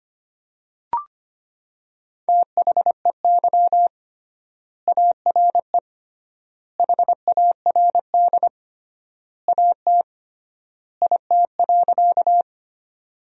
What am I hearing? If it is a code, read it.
Morse code, heard as a keyed tone, about 25 wpm, 700 Hz. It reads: THEY ARE HARD AT IT.
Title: 3x Word-Spacing - Sentences from Top 300 Words - 25wpm